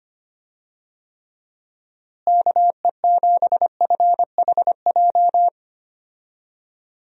Morse code, heard as a keyed tone, about 25 wpm, 700 Hz. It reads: KE7FHJ
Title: USA Call Signs - 6 Characters - Repeated 2x - 25wpm